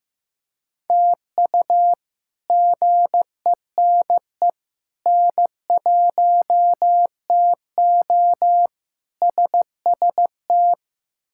Morse code, heard as a keyed tone, about 15 wpm, 700 Hz. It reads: TU GENE N1TO SST
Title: K1USN SST QSO Practice - 15wpm